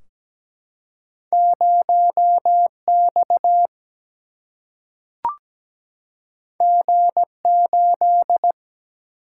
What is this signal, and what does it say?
Morse code, heard as a keyed tone, about 17 wpm, 700 Hz. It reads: 0 X G 8
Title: Single-Letter Single-Number 17wpm